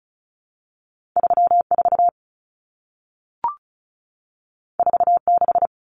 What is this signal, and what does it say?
Morse code, heard as a keyed tone, about 36 wpm, 700 Hz. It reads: 34 46
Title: Common Cut Numbers - 2-Digit 35wpm